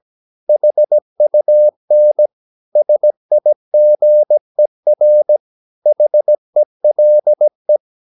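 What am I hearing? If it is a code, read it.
Morse code, heard as a keyed tone, about 17 wpm, 600 Hz. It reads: HUN SIGER HELE
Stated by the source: Slow (QRS) Morse practice at 17 wpm, from CW med Gnister.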